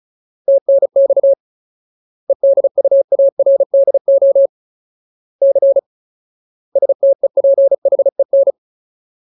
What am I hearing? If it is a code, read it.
Morse code, heard as a keyed tone, about 34 wpm, 550 Hz. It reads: TNX EDUARDO C STEPHEN
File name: Ultimate Word List / 35WPM Ultimate